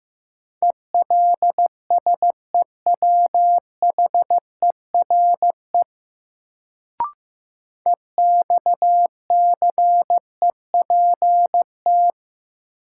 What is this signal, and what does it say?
Morse code, heard as a keyed tone, about 15 wpm, 700 Hz. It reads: ELSEWHERE EXCEPT
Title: Sentences from Top 1600 Words - New Word Drill - 15wpm